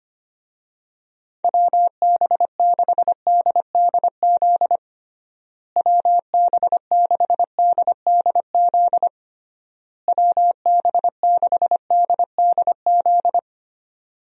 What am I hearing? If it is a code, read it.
Morse code, heard as a keyed tone, about 25 wpm, 700 Hz. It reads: WB6DDZ WB6DDZ WB6DDZ
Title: USA Call Signs - 6 Characters - Repeated 3x - 25wpm